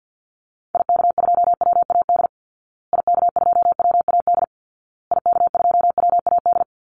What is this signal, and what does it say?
Morse code, heard as a keyed tone, about 49 wpm, 700 Hz. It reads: IK2WAD IK2WAD IK2WAD
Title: Call Signs Repeated 2x - 50wpm